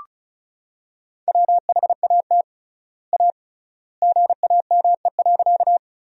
Morse code, heard as a keyed tone, about 36 wpm, 700 Hz. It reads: WHAT A GAME.
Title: 3x Word-Spacing - Sentences from Top 500 Words - 35wpm